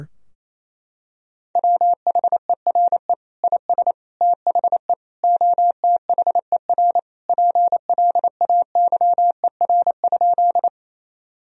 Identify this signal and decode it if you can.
Morse code, heard as a keyed tone, about 28 wpm, 700 Hz. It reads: WHERE IS THE OTHER PLAYER?
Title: Sentences from Top 1000 Words - Review Part 4 - 28wpm